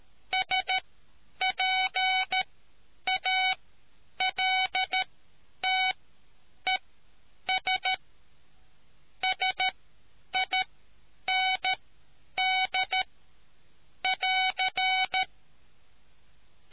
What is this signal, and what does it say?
Morse code, heard as a keyed tone, about 13 wpm, spaced as for 9 wpm, 750 Hz. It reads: SPALTES SIND <AR>